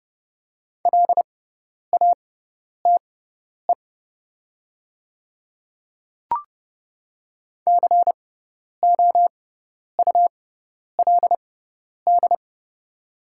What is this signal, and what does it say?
Morse code, heard as a keyed tone, about 30 wpm, spaced as for 10 wpm, 700 Hz. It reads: LATE COULD
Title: 6x Character Spacing - Top 300 Words - 30wpm